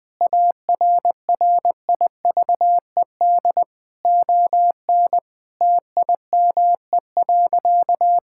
Morse code, heard as a keyed tone, about 21 wpm, 700 Hz. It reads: ARRIVED ON TIME.
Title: Sentences with 4-Word Phrases - 20wpm